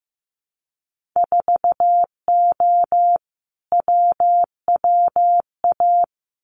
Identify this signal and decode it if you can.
Morse code, heard as a keyed tone, about 15 wpm, 700 Hz. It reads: 4O WWA
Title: Winter Field Day Exchanges 15wpm